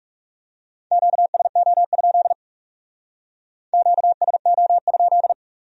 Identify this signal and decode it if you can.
Morse code, heard as a keyed tone, about 39 wpm, 700 Hz. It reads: QSO? QSO?